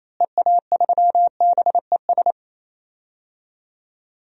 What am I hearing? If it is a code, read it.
Morse code, heard as a keyed tone, about 28 wpm, 700 Hz. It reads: EA3BES
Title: DXCC-100 Call Signs - 6 Characters - Repeated 3x - 28wpm